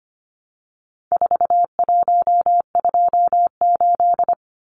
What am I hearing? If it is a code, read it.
Morse code, heard as a keyed tone, about 25 wpm, 700 Hz. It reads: 4128
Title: Numbers - 4-Digit 25wpm